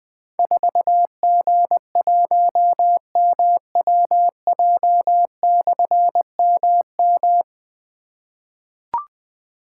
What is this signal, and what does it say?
Morse code, heard as a keyed tone, about 20 wpm, 700 Hz. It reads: 4G1MWJ/MM
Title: Difficult Call Signs 20wpm